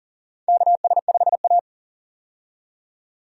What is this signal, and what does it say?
Morse code, heard as a keyed tone, about 40 wpm, 700 Hz. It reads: KS5A
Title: USA Call Signs - 4 Characters - 40wpm